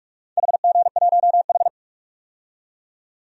Morse code, heard as a keyed tone, about 45 wpm, 700 Hz. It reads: HG1H
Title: DXCC-100 Call Signs - 4 Characters - 45wpm